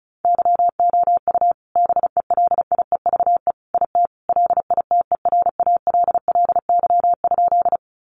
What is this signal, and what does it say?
Morse code, heard as a keyed tone, about 34 wpm, 700 Hz. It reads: YOU BELIEVE IT LITERALLY?
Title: Sentences from Top 1400 Words 35wpm